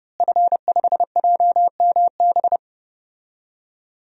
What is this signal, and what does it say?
Morse code, heard as a keyed tone, about 30 wpm, 700 Hz. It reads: F5JMB
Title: DXCC-100 Call Signs - 5 Characters - Repeated 2x - 30wpm